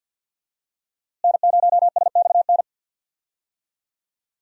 Morse code, heard as a keyed tone, about 51 wpm, 700 Hz. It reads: N0SXN
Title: USA Call Signs - 5 Characters - Repeated 2x - 50wpm